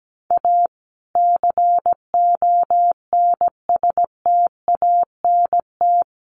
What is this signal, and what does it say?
Morse code, heard as a keyed tone, about 17 wpm, 700 Hz. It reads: A CONSTANT
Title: Sentences from Top 1500 Words 17wpm